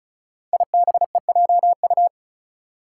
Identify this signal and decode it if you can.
Morse code, heard as a keyed tone, about 36 wpm, 700 Hz. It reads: IBEJU
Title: Five Challenging Letter-Number Combinations 35wpm